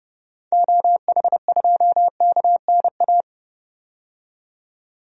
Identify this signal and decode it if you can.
Morse code, heard as a keyed tone, about 30 wpm, 700 Hz. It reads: OH2KNA
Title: DXCC-100 Call Signs - 6 Characters - 30wpm